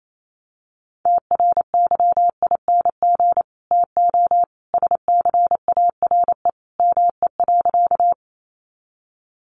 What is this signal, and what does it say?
Morse code, heard as a keyed tone, about 28 wpm, 700 Hz. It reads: TRYING TO SCARE ME.
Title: A Sound Of Thunder 28wpm